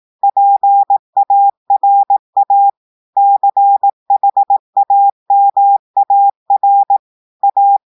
Morse code, heard as a keyed tone, about 18 wpm, 800 Hz. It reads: PARA CHAMAR A
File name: portugais texte en clair 2 vitesse 18 mots minutes